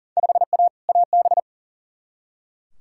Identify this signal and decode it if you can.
Morse code, heard as a keyed tone, about 40 wpm, 700 Hz. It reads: 5A AB
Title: Field Day Exchanges Encore - 40wpm